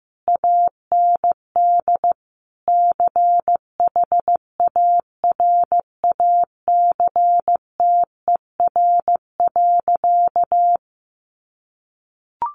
Morse code, heard as a keyed tone, about 15 wpm, 700 Hz. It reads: AND CHARACTER.
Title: Sentences from Top 1400 Words 15wpm